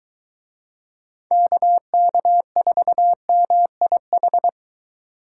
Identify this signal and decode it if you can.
Morse code, heard as a keyed tone, about 23 wpm, 700 Hz. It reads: KK4MIH